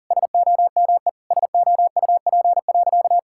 Morse code, heard as a keyed tone, about 40 wpm, 700 Hz. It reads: SOME SOUP.
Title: Sentences from Top 1000 Words - Review Part 6 - 40wpm